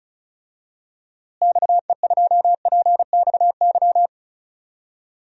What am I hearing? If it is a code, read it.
Morse code, heard as a keyed tone, about 35 wpm, 700 Hz. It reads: XE2PXY
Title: DXCC-100 Call Signs - 6 Characters - Repeated 3x - 35wpm